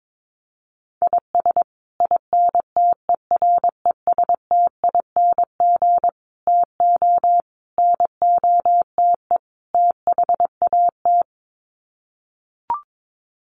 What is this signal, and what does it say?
Morse code, heard as a keyed tone, about 22 wpm, 700 Hz. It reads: IS INTERESTING TO NOTE THAT
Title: Sets of 5 Words Encore2 22wpm